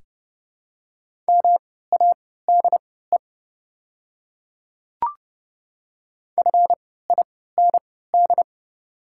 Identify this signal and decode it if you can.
Morse code, heard as a keyed tone, about 30 wpm, spaced as for 17 wpm, 690 Hz. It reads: MADE FIND
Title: 3x Character Spacing - Top 200 Words - 30wpm